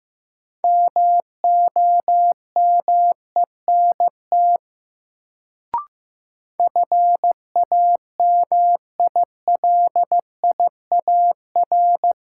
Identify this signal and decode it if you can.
Morse code, heard as a keyed tone, about 15 wpm, 700 Hz. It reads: MOMENT FAMILIAR
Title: Sentences from Top 1000 Words - New Word Drill - 15wpm